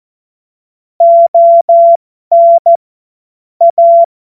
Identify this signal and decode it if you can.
Morse code, heard as a keyed tone, about 14 wpm, spaced as for 12 wpm, 680 Hz. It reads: ON A
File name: Morsekurs / Witze